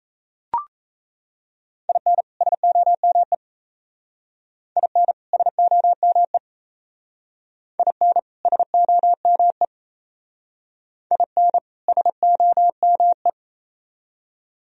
Morse code, heard as a keyed tone, about 34 wpm, 700 Hz. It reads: IN SOME IN SOME IN SOME IN SOME
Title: Speed-Racing - Sets of 2 Words 28wpm